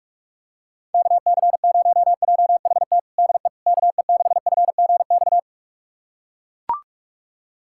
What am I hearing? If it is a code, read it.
Morse code, heard as a keyed tone, about 41 wpm, 700 Hz. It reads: KC0JHT DE KE6FGX
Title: USA Callsign DE Callsign - 6 Characters - 45wpm